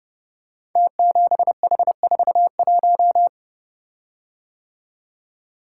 Morse code, heard as a keyed tone, about 28 wpm, 700 Hz. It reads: T7H41
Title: Code Groups - 5 Letter:Numbers - 30wpm